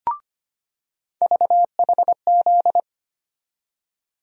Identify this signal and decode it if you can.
Morse code, heard as a keyed tone, about 25 wpm, 700 Hz. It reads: VHZ
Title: USA Call Sign Postfixes - 3 Letters 25wpm